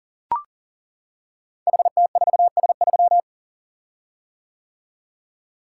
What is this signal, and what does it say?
Morse code, heard as a keyed tone, about 39 wpm, 690 Hz. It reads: HT4S3
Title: Code Groups - 5 Letter:Numbers - 40wpm